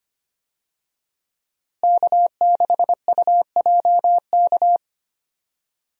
Morse code, heard as a keyed tone, about 25 wpm, 700 Hz. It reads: K6UJK
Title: USA Call Signs - 5 Characters - Repeated 2x - 25wpm